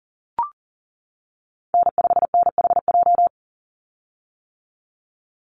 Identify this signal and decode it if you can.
Morse code, heard as a keyed tone, about 40 wpm, 700 Hz. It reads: N5NHJ
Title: Call Signs - 40wpm